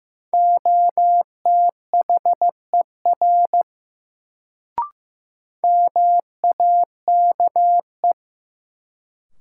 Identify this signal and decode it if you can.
Morse code, heard as a keyed tone, about 15 wpm, 700 Hz. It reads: OTHER MAKE
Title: Sentences from Top 100 Words - New Word Drill - 15wpm